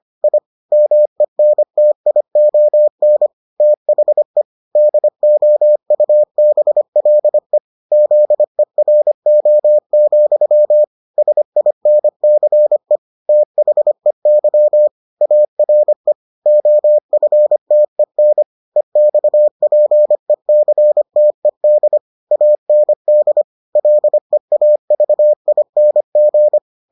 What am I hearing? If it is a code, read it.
Morse code, heard as a keyed tone, about 25 wpm, 600 Hz. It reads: I MENTION THE DOUBLE ZERO, SINCE THEY ARE OFTEN EXPECTED AND LEAVING